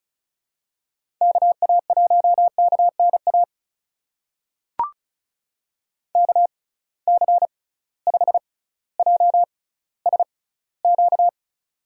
Morse code, heard as a keyed tone, about 34 wpm, spaced as for 18 wpm, 700 Hz. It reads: KA1KNA KC5JSQ
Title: USA Call Signs - 6 Characters - 6x Character Spacing - 35wpm